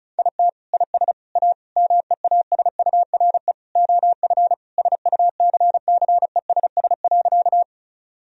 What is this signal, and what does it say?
Morse code, heard as a keyed tone, about 34 wpm, 700 Hz. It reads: IT IS A MEASURE OF SUCCESS.
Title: Sentences from Top 1000 Words - Review Part 2 - 35wpm